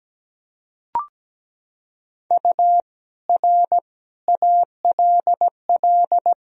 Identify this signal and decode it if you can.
Morse code, heard as a keyed tone, about 17 wpm, 700 Hz. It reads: U R ALL
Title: Sentences from Top 200 Words and Abbreviations 17wpm